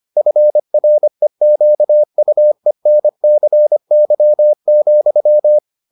Morse code, heard as a keyed tone, about 25 wpm, 600 Hz. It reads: FREQUENCY,